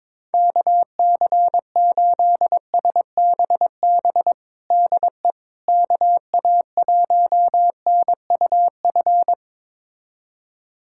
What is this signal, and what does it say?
Morse code, heard as a keyed tone, about 22 wpm, 700 Hz. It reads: KC8SBB DE KA1NUF
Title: USA Callsign DE Callsign - 6 Characters - 22wpm